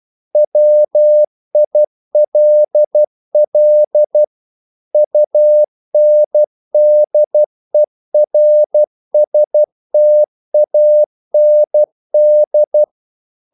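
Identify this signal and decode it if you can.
Morse code, heard as a keyed tone, about 12 wpm, 600 Hz. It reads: WILL UNDERSTAND
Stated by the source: Morse practice sending English quotations.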